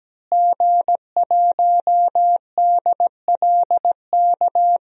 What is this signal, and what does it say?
Morse code, heard as a keyed tone, about 17 wpm, 700 Hz. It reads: G1DLK
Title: DXCC-100 Call Signs - 5 Characters - Repeated 2x - 17wpm